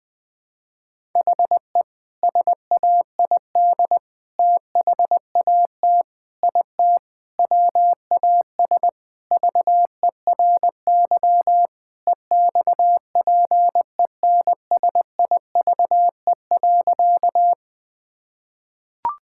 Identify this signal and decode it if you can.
Morse code, heard as a keyed tone, about 21 wpm, 700 Hz. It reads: HE SAID THAT IT WAS VERY EXPENSIVE.
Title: Sentences from Top 1000 Words - Review Part 6 - 20wpm